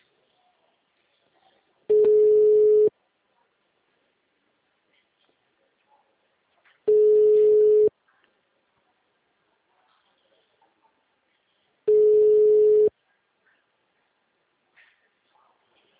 Ringback tone (Europe, 425 Hz).
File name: out- -2019- - - .19774